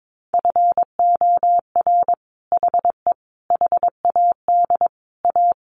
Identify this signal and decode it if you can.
Morse code, heard as a keyed tone, about 22 wpm, 700 Hz. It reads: FOR HE HAD A
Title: Sentences from Top 1400 Words 22wpm